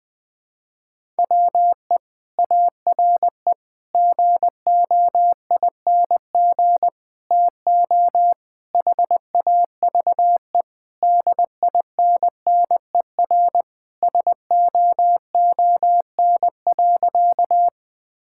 Morse code, heard as a keyed tone, about 20 wpm, 700 Hz. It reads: WE ARE GOING TO HAVE DINNER SOON.
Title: Sentences with 3-Word Phrases - Spoken Beginning - 20wpm